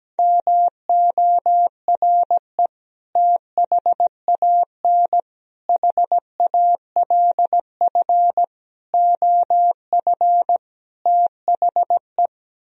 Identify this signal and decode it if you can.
Morse code, heard as a keyed tone, about 17 wpm, 700 Hz. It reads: MORE THAN HALF OF THE
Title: Speed-Racing - Sets of 5 Words 17wpm